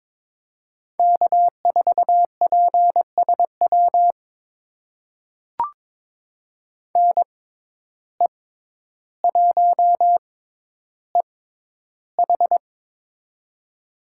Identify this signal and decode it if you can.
Morse code, heard as a keyed tone, about 22 wpm, spaced as for 11 wpm, 700 Hz. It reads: K4PSW NE1EH
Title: USA Call Signs - 5 Characters - 6x Character Spacing - 22wpm